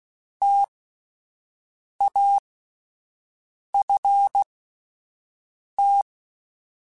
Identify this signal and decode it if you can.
Morse code, heard as a keyed tone, about 16 wpm, spaced as for 5 wpm, 800 Hz. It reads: TAFT
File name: Alphabetical / Alphabetical mp3s / Ts @5